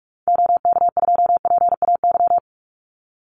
Morse code, heard as a keyed tone, about 46 wpm, 700 Hz. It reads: OK2PAY
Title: Call Signs Repeated 3x - 45wpm